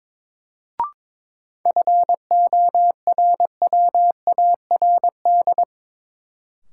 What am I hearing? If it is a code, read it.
Morse code, heard as a keyed tone, about 23 wpm, 700 Hz. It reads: FORWARD
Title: Sentences from Top 1300 Words - New Word Drill - 22wpm